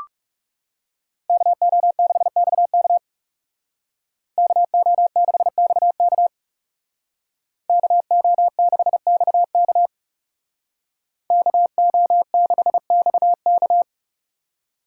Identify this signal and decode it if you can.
Morse code, heard as a keyed tone, about 37 wpm, 700 Hz. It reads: KO6XK KO6XK KO6XK KO6XK
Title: Speed Racing - USA 2x2 Call Signs 30wpm